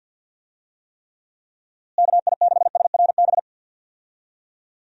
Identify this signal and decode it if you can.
Morse code, heard as a keyed tone, about 51 wpm, 700 Hz. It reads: KI6SRB
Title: USA Call Signs - 6 Characters - Repeated 3x - 50wpm